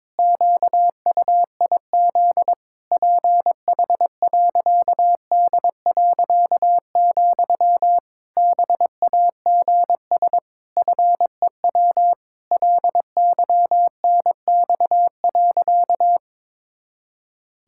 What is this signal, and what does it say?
Morse code, heard as a keyed tone, about 22 wpm, 700 Hz. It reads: QUIZ PH.D., BAGS FEW LYNX.
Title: Pangram Challenge v2 - 22wpm